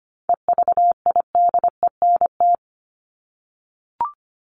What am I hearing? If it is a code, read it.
Morse code, heard as a keyed tone, about 25 wpm, 700 Hz. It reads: EVIDENT
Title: LY Suffix - Top 100 Words with Base Words 25wpm